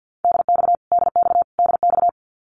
Morse code, heard as a keyed tone, about 50 wpm, 700 Hz. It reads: DX DX DX
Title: Abbreviations Repeated 3x - 50wpm